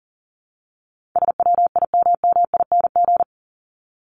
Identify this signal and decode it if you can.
Morse code, heard as a keyed tone, about 40 wpm, 700 Hz. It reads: SWIMMING
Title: Top 1501-1750 Words 40wpm